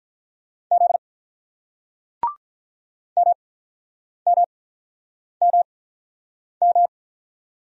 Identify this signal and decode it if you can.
Morse code, heard as a keyed tone, about 47 wpm, spaced as for 9 wpm, 700 Hz. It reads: Z MMMM